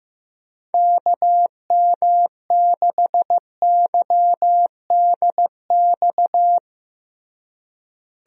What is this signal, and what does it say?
Morse code, heard as a keyed tone, about 15 wpm, 700 Hz. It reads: KM6YDX
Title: USA Call Signs - 6 Characters - Repeated 3x - 15wpm